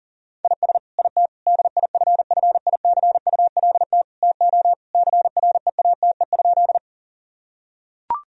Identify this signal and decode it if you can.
Morse code, heard as a keyed tone, about 40 wpm, 700 Hz. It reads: IS IT DIFFICULT TO CREATE?
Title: Sentences from Top 1000 Words - Review Part 3 - 40wpm